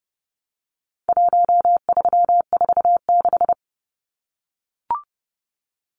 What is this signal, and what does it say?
Morse code, heard as a keyed tone, about 30 wpm, 700 Hz. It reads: 1346
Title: Numbers - Contest Style - 4-Digit 30wpm